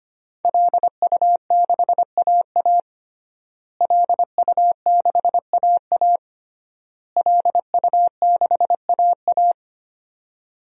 Morse code, heard as a keyed tone, about 25 wpm, 700 Hz. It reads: LU6AA LU6AA LU6AA
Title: DXCC-100 Call Signs - 5 Characters - Repeated 3x - 25wpm